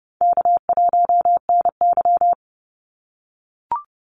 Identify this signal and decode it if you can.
Morse code, heard as a keyed tone, about 30 wpm, 700 Hz. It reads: K1NY
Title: Easy Call Signs - 30wpm